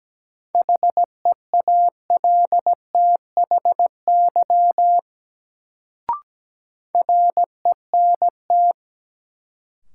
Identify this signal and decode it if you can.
Morse code, heard as a keyed tone, about 17 wpm, 700 Hz. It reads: HEALTHY RENT
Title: Sentences from Top 1000 Words - Review Part 5 - New Word Drill - 17wpm